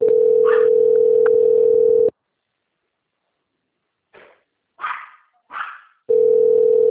Ringback tone (North America), 440 + 480 Hz.